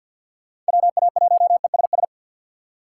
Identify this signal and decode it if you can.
Morse code, heard as a keyed tone, about 48 wpm, 700 Hz. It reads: WA1ESS
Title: USA Call Signs - 6x Character Spacing - 50wpm